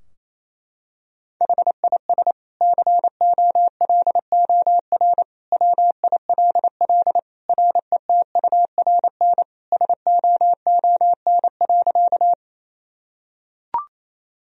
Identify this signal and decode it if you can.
Morse code, heard as a keyed tone, about 28 wpm, 700 Hz. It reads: HIS COLOR WILL RETURN SOON.
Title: Sentences from Top 1000 Words - Review Part 2 - 28wpm